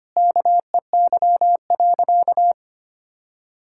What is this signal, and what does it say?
Morse code, heard as a keyed tone, about 25 wpm, 700 Hz. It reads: KEY.
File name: practice / 25wpm / Sentences with 4-Word Phrases - 25wpm